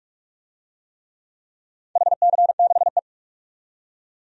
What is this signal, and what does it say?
Morse code, heard as a keyed tone, about 47 wpm, 700 Hz. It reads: HC6E
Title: DXCC-100 Call Signs - 4 Characters - 45wpm